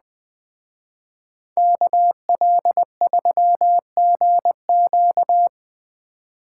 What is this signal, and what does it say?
Morse code, heard as a keyed tone, about 20 wpm, 700 Hz. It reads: KL3GQ